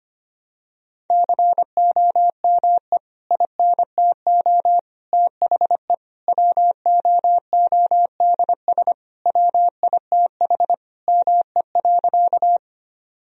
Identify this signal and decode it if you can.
Morse code, heard as a keyed tone, about 25 wpm, 700 Hz. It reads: COME INTO THE WOODS WITH ME.
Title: Sentences from Top 1000 Words - Review Part 2 - 25wpm